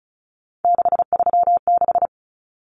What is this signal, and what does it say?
Morse code, heard as a keyed tone, about 34 wpm, 700 Hz. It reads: 636
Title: Common Cut Numbers - 3-Digit 35wpm